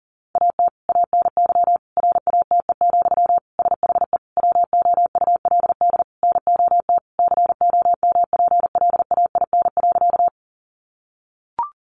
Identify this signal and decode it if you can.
Morse code, heard as a keyed tone, about 40 wpm, 700 Hz. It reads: AT ANY RATE, SHE WOULD NOT COMPLAIN.
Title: Sentences from Top 1300 Words 40wpm